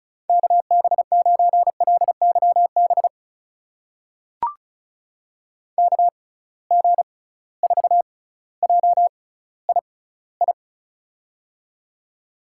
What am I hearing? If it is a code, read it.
Morse code, heard as a keyed tone, about 36 wpm, spaced as for 18 wpm, 700 Hz. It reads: KB9LYB KG4JII
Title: USA Call Signs - 6 Characters - 6x Character Spacing - 35wpm